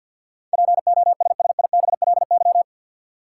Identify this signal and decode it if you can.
Morse code, heard as a keyed tone, about 53 wpm, 700 Hz. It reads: POSSIBLY
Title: Sentences from Top 1300 Words - New Word Drill Alt - 50wpm